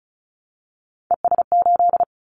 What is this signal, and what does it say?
Morse code, heard as a keyed tone, about 34 wpm, 700 Hz. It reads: ES8
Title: Callsign Prefixes for Top 100 DXCC - 35wpm